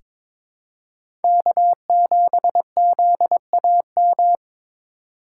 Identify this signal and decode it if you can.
Morse code, heard as a keyed tone, about 22 wpm, 700 Hz. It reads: K7ZAM
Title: USA Call Signs - 5 Characters - 4x Character Spacing - 22wpm